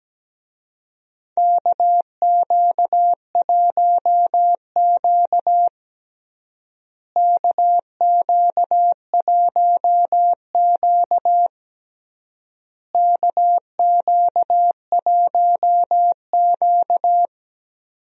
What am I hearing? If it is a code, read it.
Morse code, heard as a keyed tone, about 17 wpm, 700 Hz. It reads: KQ1Q KQ1Q KQ1Q
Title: USA Call Signs - 4 Characters - Repeated 3x - 17wpm